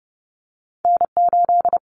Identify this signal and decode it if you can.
Morse code, heard as a keyed tone, about 30 wpm, 700 Hz. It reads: N8
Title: Common Cut Numbers - 2-Digit 30wpm